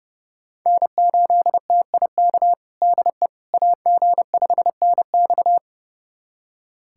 Morse code, heard as a keyed tone, about 29 wpm, 700 Hz. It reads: N8TIK DE AG5NX
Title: USA Callsign DE Callsign - 5 Characters - 30wpm